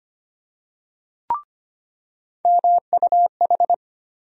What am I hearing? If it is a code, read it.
Morse code, heard as a keyed tone, about 25 wpm, 700 Hz. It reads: MUH
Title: USA Call Sign Postfixes - 3 Letters 25wpm